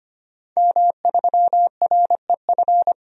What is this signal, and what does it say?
Morse code, heard as a keyed tone, about 25 wpm, 700 Hz. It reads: M3REF
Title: DXCC-100 Call Signs - 5 Characters - 25wpm